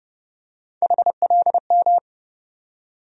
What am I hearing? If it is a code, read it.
Morse code, heard as a keyed tone, about 30 wpm, 700 Hz. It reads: HLM
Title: USA Call Sign Postfixes - 3 Letters 30wpm